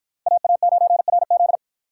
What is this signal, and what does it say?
Morse code, heard as a keyed tone, about 53 wpm, 700 Hz. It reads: AA9RZ